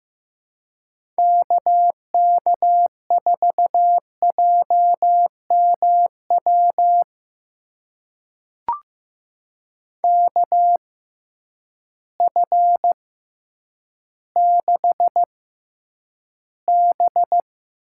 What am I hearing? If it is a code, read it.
Morse code, heard as a keyed tone, about 15 wpm, spaced as for 9 wpm, 700 Hz. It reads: KK4JMW KF6B
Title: USA Call Signs - 6 Characters - 6x Character Spacing - 15wpm